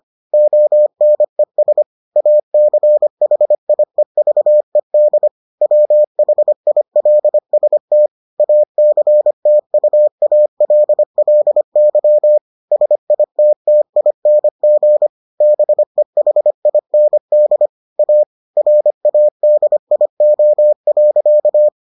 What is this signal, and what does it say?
Morse code, heard as a keyed tone, about 25 wpm, 600 Hz. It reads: ONES ACHIEVED WHILST ACTUALLY SITTING BEHIND A RADIO.